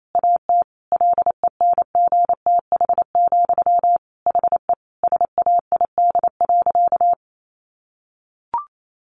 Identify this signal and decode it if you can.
Morse code, heard as a keyed tone, about 28 wpm, 700 Hz. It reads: AT LENGTH, HE SAID.
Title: Sentences from Top 1000 Words 28wpm